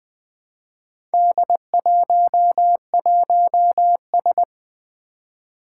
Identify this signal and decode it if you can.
Morse code, heard as a keyed tone, about 20 wpm, 700 Hz. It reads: D11S
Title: Four Challenging Letter-Number Combinations 20wpm